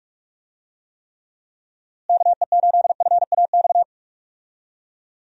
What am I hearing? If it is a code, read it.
Morse code, heard as a keyed tone, about 46 wpm, 700 Hz. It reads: KE8FAX
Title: USA Call Signs - 6 Characters - Repeated 3x - 45wpm